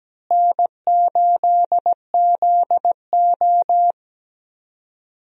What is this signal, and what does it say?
Morse code, heard as a keyed tone, about 17 wpm, 700 Hz. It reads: N8ZO
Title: USA Call Signs - 4 Characters - Repeated 2x - 17wpm